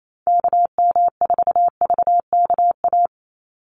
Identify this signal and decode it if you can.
Morse code, heard as a keyed tone, about 28 wpm, 700 Hz. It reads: KM4VKA
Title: USA Call Signs 28wpm